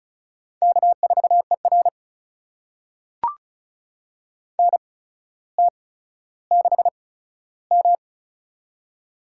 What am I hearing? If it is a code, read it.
Morse code, heard as a keyed tone, about 36 wpm, spaced as for 15 wpm, 700 Hz. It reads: K4ER NT6M
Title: USA Call Signs - 4 Characters - 8x Character Spacing - 35wpm